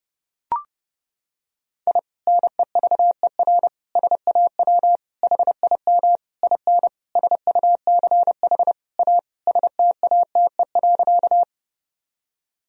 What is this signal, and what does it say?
Morse code, heard as a keyed tone, about 31 wpm, 700 Hz. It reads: I NEVER SAW HIM IN SUCH A STATE.
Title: Sentences from Top 1000 Words - Review Part 2 - 30wpm